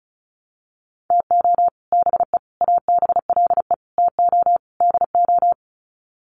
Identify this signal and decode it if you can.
Morse code, heard as a keyed tone, about 34 wpm, 700 Hz. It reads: TO BE ABLE TO DO
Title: Sets of 5 Words 35wpm